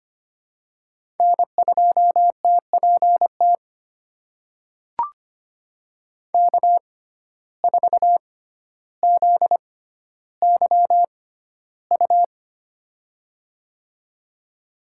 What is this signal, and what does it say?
Morse code, heard as a keyed tone, about 25 wpm, spaced as for 13 wpm, 700 Hz. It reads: N2TPT K4ZYU
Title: USA Call Signs - 5 Characters - 6x Character Spacing - 25wpm